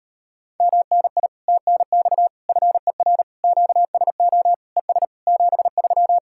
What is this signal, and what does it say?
Morse code, heard as a keyed tone, about 33 wpm, 700 Hz. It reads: MNI TNX FER QSO ES 73